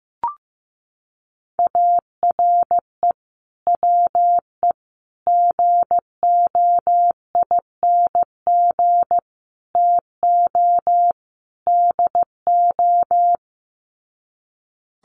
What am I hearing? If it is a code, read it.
Morse code, heard as a keyed tone, about 15 wpm, 700 Hz. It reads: ARE WE GOING TO DO
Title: Sets of 5 Words 15wpm